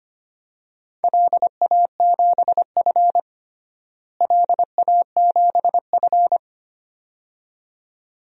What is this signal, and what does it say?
Morse code, heard as a keyed tone, about 25 wpm, 700 Hz. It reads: LA7F LA7F
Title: DXCC-100 Call Signs - 4 Characters - Repeated 2x - 25wpm